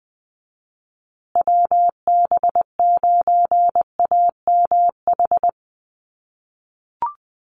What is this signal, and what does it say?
Morse code, heard as a keyed tone, about 20 wpm, 700 Hz. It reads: WB9AMH
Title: USA Call Signs Repeated 3x 20wpm